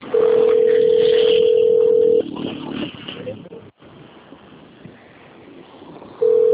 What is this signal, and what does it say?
Ringback tone (North America), 440 + 480 Hz.